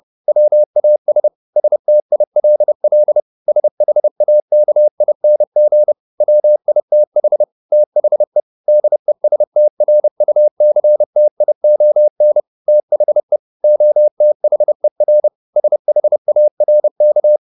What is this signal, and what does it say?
Morse code, heard as a keyed tone, about 30 wpm, 600 Hz. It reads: WAS STILL SHAKING WITH THE DESTRUCTION THE OTHER SHARK